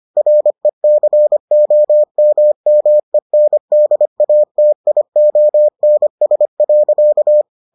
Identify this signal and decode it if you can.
Morse code, heard as a keyed tone, about 25 wpm, 600 Hz. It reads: RECOMMENDATIONS.